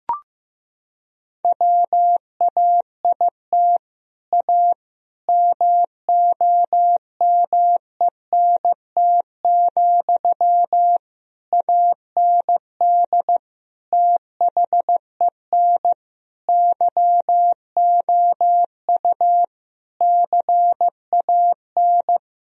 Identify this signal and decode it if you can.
Morse code, heard as a keyed tone, about 15 wpm, 700 Hz. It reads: WAIT A MOMENT, AND THEN YOU CAN
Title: Sentences from Top 1000 Words - Review Part 4 - 15wpm